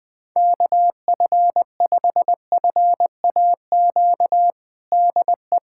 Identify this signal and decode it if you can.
Morse code, heard as a keyed tone, about 20 wpm, 700 Hz. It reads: KF5FAQ DE
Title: USA Callsign DE Callsign 20wpm